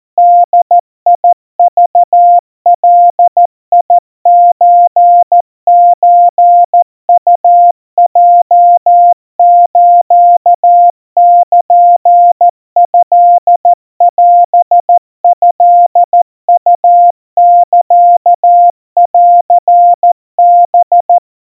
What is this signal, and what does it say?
Morse code (Japanese Wabun code), heard as a keyed tone, about 14 wpm, 700 Hz. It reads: ボクガソソウヲスルトオトウサンハ
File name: kawa / Morse sample / kamyu 50 cpm 700 Hz Crispy